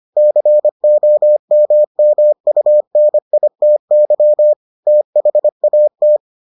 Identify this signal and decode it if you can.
Morse code, heard as a keyed tone, about 25 wpm, 600 Hz. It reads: COMMUNITY THAT